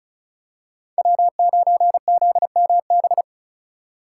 Morse code, heard as a keyed tone, about 36 wpm, 700 Hz. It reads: W9ZMB
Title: USA Call Signs - 5 Characters - 2x Character Spacing - 35wpm